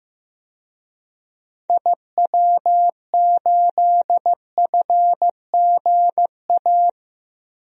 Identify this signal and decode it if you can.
Morse code, heard as a keyed tone, about 15 wpm, 700 Hz. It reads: IW8FGA
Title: DXCC-100 Call Signs - 6 Characters - Repeated 3x - 15wpm